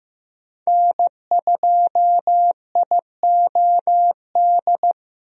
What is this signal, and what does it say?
Morse code, heard as a keyed tone, about 15 wpm, 700 Hz. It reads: N2IOD
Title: USA Call Signs - 5 Characters - 15wpm